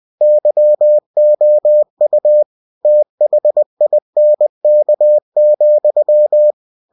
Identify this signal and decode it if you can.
Morse code, heard as a keyed tone, about 20 wpm, 600 Hz. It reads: YOU THINK,
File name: cw-books / winnie-the-pooh / winnie-20wpm-page-0085